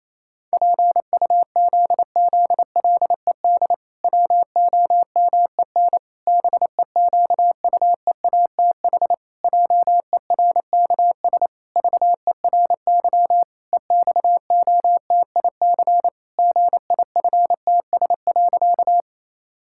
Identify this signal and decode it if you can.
Morse code, heard as a keyed tone, about 28 wpm, 700 Hz. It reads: PUZZLED WOMEN BEQUEATH JERKS VERY EXOTIC GIFTS.
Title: Pangram Challenge v2 - 28wpm